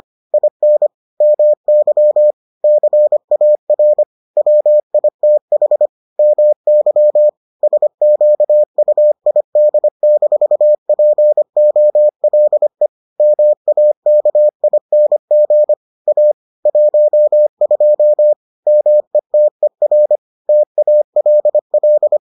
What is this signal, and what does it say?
Morse code, heard as a keyed tone, about 25 wpm, 600 Hz. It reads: IN MY CAR WITH MY SQUID-POLE MAKING A 12 METER TALL